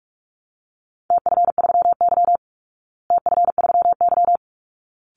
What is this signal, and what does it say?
Morse code, heard as a keyed tone, about 44 wpm, 700 Hz. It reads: TF3Y TF3Y
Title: Call Signs Repeated 3x - 45wpm